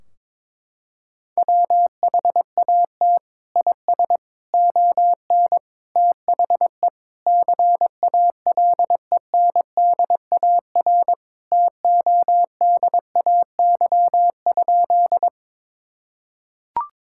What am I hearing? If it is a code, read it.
Morse code, heard as a keyed tone, about 22 wpm, 700 Hz. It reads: WHAT IS ON THE CALENDAR TODAY?
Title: Sentences from Top 1600 Words 22wpm